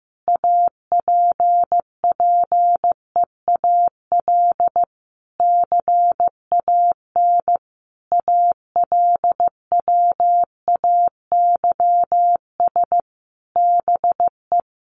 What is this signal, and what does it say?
Morse code, heard as a keyed tone, about 15 wpm, 700 Hz. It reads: APPEAL CAN ALWAYS BE
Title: Sentences from Top 1300 Words 15wpm